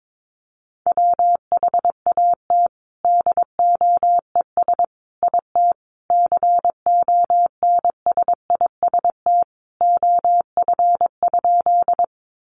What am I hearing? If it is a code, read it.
Morse code, heard as a keyed tone, about 22 wpm, 700 Hz. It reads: WHAT DOES IT CONSIST OF?
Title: Sentences from Top 1400 Words 22wpm